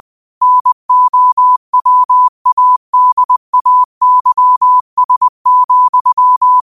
Morse code, heard as a keyed tone, about 20 wpm, 1000 Hz. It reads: NOWADAYS,
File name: qotdfiles / 20wpm